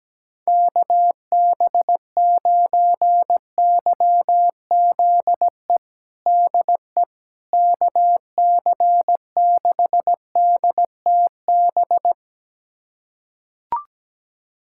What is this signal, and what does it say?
Morse code, heard as a keyed tone, about 17 wpm, 700 Hz. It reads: KB9YZE DE KC6DTB
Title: USA Callsign DE Callsign - 6 Characters - 17wpm